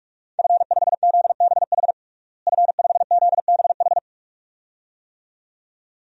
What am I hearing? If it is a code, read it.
Morse code, heard as a keyed tone, about 46 wpm, 700 Hz. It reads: F5ZBH F5ZBH
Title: DXCC-100 Call Signs - 5 Characters - Repeated 2x - 45wpm